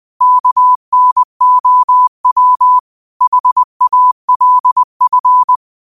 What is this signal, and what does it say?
Morse code, heard as a keyed tone, about 20 wpm, 1000 Hz. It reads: KNOW HALF
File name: qotdfiles / 20wpm